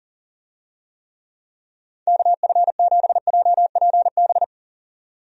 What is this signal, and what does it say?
Morse code, heard as a keyed tone, about 38 wpm, 700 Hz. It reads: KF7JPB
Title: USA Call Signs - 6 Characters - 40wpm